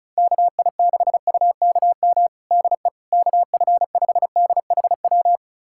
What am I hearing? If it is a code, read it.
Morse code, heard as a keyed tone, about 36 wpm, 700 Hz. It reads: KI6UKM DE KF5DHW
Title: USA Callsign DE Callsign - 6 Characters - 35wpm